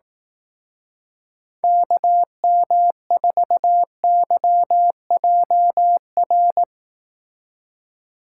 Morse code, heard as a keyed tone, about 18 wpm, 700 Hz. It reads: KM4YJR
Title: Speed Racing - USA 2x3 Call Signs 15wpm